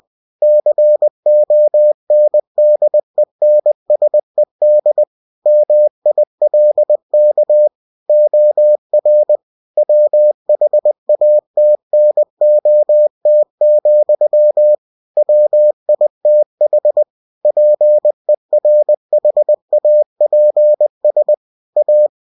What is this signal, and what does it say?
Morse code, heard as a keyed tone, about 20 wpm, 600 Hz. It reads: CONDENSED MILK OR WHATNOT, WITH PERHAPS A
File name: cw-books / winnie-the-pooh / winnie-20wpm-page-0030